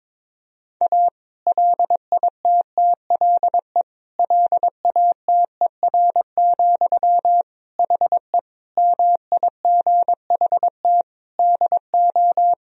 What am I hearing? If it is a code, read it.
Morse code, heard as a keyed tone, about 22 wpm, 700 Hz. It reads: A LITTLE LATER, HE MIGHT DO
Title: Sentences from Top 1000 Words - Review Part 4 - 22wpm